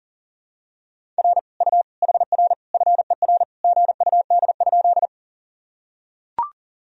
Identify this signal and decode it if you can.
Morse code, heard as a keyed tone, about 40 wpm, 700 Hz. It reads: R U HR FER GUD?
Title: Sentences from Top 200 Words and Abbreviations 40wpm